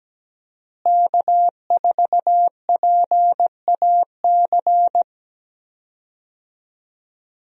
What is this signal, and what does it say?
Morse code, heard as a keyed tone, about 17 wpm, 700 Hz. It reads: K4PAC
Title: USA Call Signs - 5 Characters - Repeated 3x - 17wpm